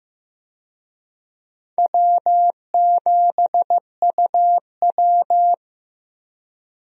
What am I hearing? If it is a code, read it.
Morse code, heard as a keyed tone, about 15 wpm, 700 Hz. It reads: W7UW
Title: USA Call Signs - 4 Characters - 15wpm